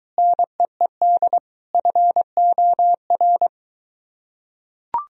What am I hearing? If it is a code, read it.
Morse code, heard as a keyed tone, about 22 wpm, 700 Hz. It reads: NEED FOR